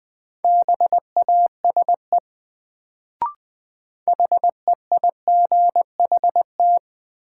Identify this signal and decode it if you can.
Morse code, heard as a keyed tone, about 20 wpm, 700 Hz. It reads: BASE HEIGHT
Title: Sentences from Top 1600 Words - New Word Drill Alt - 20wpm